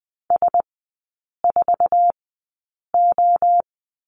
Morse code, heard as a keyed tone, about 20 wpm, spaced as for 8 wpm, 700 Hz. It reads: S4O
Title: 053 Lesson 10e - Letter r - Three Characters 20wpm